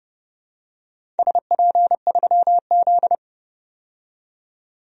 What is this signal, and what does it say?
Morse code, heard as a keyed tone, about 32 wpm, 700 Hz. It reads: SP3Z
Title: DXCC-100 Call Signs - 4 Characters - 30wpm